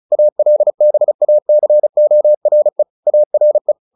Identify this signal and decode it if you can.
Morse code, heard as a keyed tone, about 34 wpm, 600 Hz. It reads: ALBACORE ARE
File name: the old man and the sea / the old man and the sea 1 35wpm